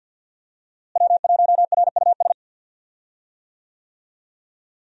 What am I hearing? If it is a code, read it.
Morse code, heard as a keyed tone, about 46 wpm, 700 Hz. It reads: W1RUS
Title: USA Call Signs - 5 Characters - Repeated 2x - 50wpm